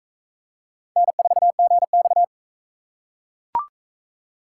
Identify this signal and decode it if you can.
Morse code, heard as a keyed tone, about 36 wpm, 700 Hz. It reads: N4GX